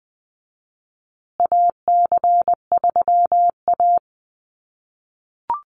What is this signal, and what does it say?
Morse code, heard as a keyed tone, about 20 wpm, 700 Hz. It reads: AC3A
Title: USA Call Signs Repeated 3x 20wpm